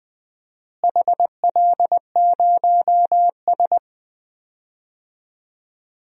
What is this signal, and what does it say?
Morse code, heard as a keyed tone, about 20 wpm, 700 Hz. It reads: HL0S
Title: DXCC-100 Call Signs - 4 Characters - Repeated 2x - 20wpm